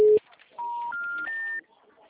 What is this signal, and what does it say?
Special information tone, with speech.